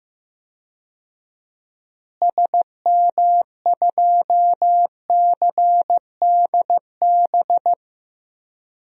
Morse code, heard as a keyed tone, about 16 wpm, 700 Hz. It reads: SM2CDB
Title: DXCC-100 Call Signs - 6 Characters - Repeated 2x - 15wpm